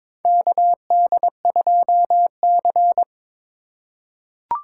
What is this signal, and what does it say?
Morse code, heard as a keyed tone, about 22 wpm, 700 Hz. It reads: KD2C
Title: USA Call Signs - 2x Character Spacing - 22wpm